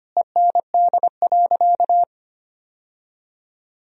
Morse code, heard as a keyed tone, about 25 wpm, 700 Hz. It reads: END.
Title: Sentences from Top 1000 Words - Review Part 3 - 25wpm